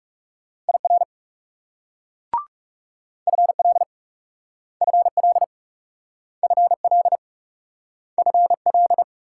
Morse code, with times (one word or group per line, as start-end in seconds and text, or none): 0.7-1.0 IR
3.3-3.8 FL
4.8-5.4 FL
6.4-7.2 FL
8.2-9.0 FL